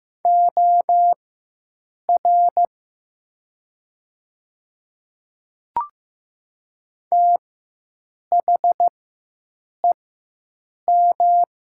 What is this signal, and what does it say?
Morse code, heard as a keyed tone, about 15 wpm, spaced as for 7 wpm, 700 Hz. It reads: OR THEM
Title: 4x Character Spacing - Top 100 Words - 15wpm